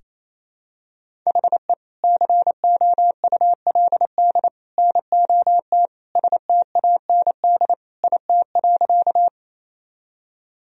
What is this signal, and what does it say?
Morse code, heard as a keyed tone, about 28 wpm, 700 Hz. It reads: HE COULD NOT STAND IT.
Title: Sentences from Top 1000 Words - Review Part 1 - 28wpm